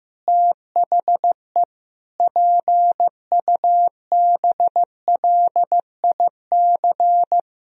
Morse code, heard as a keyed tone, about 16 wpm, 700 Hz. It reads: THE PUBLIC
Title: Sentences from Top 1000 Words - Review Part 3 - 15wpm